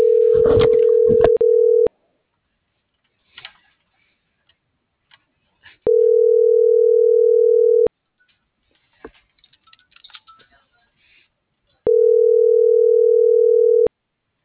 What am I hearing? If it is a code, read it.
Ringback tone (North America), 440 + 480 Hz.